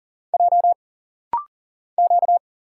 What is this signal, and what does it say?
Morse code, heard as a keyed tone, about 39 wpm, 700 Hz. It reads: J Q
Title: Single Letters - Rapid-Fire 40wpm